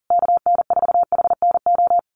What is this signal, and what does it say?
Morse code, heard as a keyed tone, about 37 wpm, 700 Hz. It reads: KN4HNO
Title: USA Call Signs Repeated 3x 40wpm